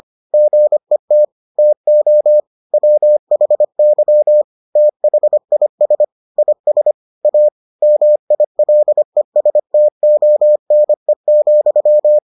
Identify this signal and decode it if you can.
Morse code, heard as a keyed tone, about 25 wpm, 600 Hz. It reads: GET TO WHY THIS IS A MILESTONE,